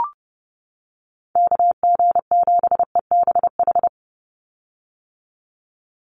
Morse code, heard as a keyed tone, about 30 wpm, 700 Hz. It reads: KG7EBH
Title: USA Call Signs 30wpm